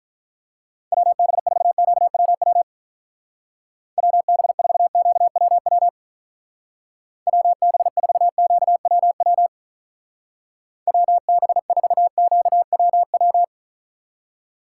Morse code, heard as a keyed tone, about 43 wpm, 700 Hz. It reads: WB4QWW WB4QWW WB4QWW WB4QWW
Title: Speed Racing - USA 2x3 Call Signs 35wpm